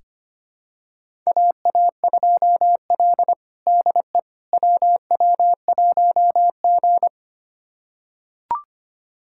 Morse code, heard as a keyed tone, about 25 wpm, 700 Hz. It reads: AA2L DE WW1G
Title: USA Callsign DE Callsign - 4 Characters - 25wpm